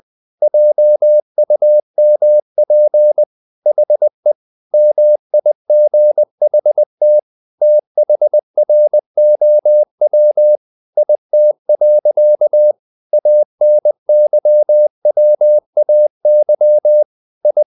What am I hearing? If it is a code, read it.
Morse code, heard as a keyed tone, about 20 wpm, 600 Hz. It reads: JUMP HE MIGHT THROW IT. ANYWAY I
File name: the old man and the sea / the old man and the sea part 1 20wpm